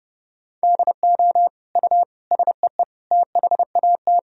Morse code, heard as a keyed tone, about 30 wpm, 700 Hz. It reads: DO U SEE THAT
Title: Sentences from Top 500 Words and Abbreviations 30wpm